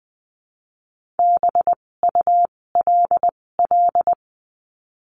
Morse code, heard as a keyed tone, about 20 wpm, spaced as for 16 wpm, 700 Hz. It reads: BULL
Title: 256 Lesson 42a - Words 12-20wpm